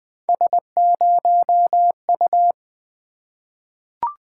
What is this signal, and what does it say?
Morse code, heard as a keyed tone, about 20 wpm, 700 Hz. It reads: S0U
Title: Three Letter-Number Combinations - 20wpm